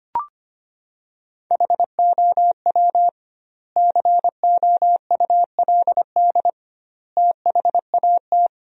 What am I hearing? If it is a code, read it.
Morse code, heard as a keyed tone, about 25 wpm, 700 Hz. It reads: HOW COULD THAT
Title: 2x Word-Spacing - Sentences from Top 400 Words - 25wpm